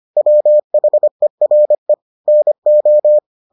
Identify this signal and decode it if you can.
Morse code, heard as a keyed tone, about 25 wpm, 600 Hz. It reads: WHERE NO